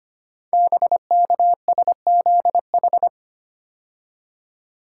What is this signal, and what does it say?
Morse code, heard as a keyed tone, about 25 wpm, 700 Hz. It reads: BKSZH
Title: Code Groups - 5 Letters - 25wpm